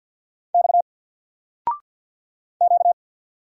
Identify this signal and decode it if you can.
Morse code, heard as a keyed tone, about 48 wpm, 700 Hz. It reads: X Q
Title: Single Letter-Number Alt 50wpm